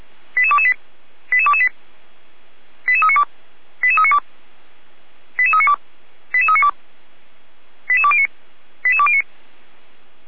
Five-tone selective call (ZVEI), heard as quick runs of five tones, tones of about 70 ms. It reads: 88228 88228 88392 88392 88392 88392 88209 88209